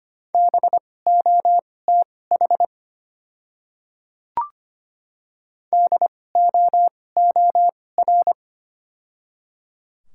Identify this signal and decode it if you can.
Morse code, heard as a keyed tone, about 25 wpm, spaced as for 18 wpm, 700 Hz. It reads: BOTH DOOR
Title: 2x Character Spacing - Top 300 Words - 25wpm